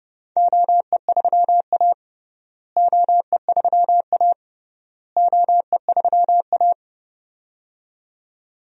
Morse code, heard as a keyed tone, about 30 wpm, 700 Hz. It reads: OE3A OE3A OE3A
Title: DXCC-100 Call Signs - 4 Characters - Repeated 3x - 30wpm